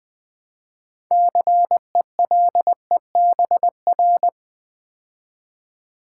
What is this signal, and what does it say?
Morse code, heard as a keyed tone, about 20 wpm, 700 Hz. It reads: CELEBR
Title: Build A Word - 3 Letters At A Time - 9-Letter Words - 20wpm